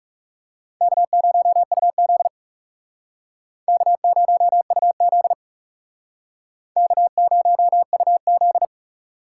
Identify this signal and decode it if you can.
Morse code, heard as a keyed tone, about 36 wpm, 700 Hz. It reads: K0UZ K0UZ K0UZ
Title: Speed Racing - USA 1x2 Call Signs 30wpm